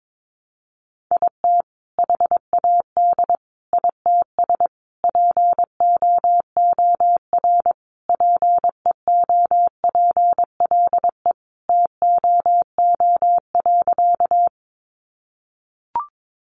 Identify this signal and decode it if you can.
Morse code, heard as a keyed tone, about 22 wpm, 700 Hz. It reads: IT HAD ITS POOR PEOPLE TOO.
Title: Sentences from Top 1000 Words 22wpm